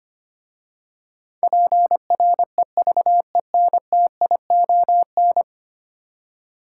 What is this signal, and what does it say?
Morse code, heard as a keyed tone, about 25 wpm, 700 Hz. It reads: PREVENTION
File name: 10 Letter Words - First 3 Letters Given - 25wpm